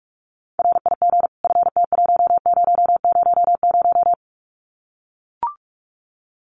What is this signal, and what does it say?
Morse code, heard as a keyed tone, about 44 wpm, 700 Hz. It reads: RIG FT1000
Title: Common QSO Phrases 45wpm